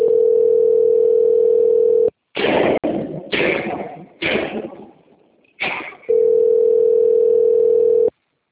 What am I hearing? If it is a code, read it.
Ringback tone (North America), 440 + 480 Hz.
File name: rg-1600-unknown- - - .8663